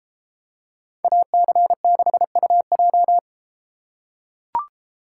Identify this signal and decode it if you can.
Morse code, heard as a keyed tone, about 32 wpm, 700 Hz. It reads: AC6UJ